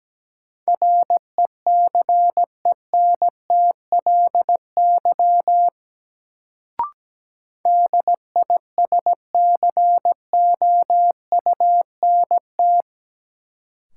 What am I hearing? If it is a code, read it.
Morse code, heard as a keyed tone, about 17 wpm, 700 Hz. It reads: RECENTLY DISCOUNT
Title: Sentences from Top 1200 Words - New Word Drill - 17wpm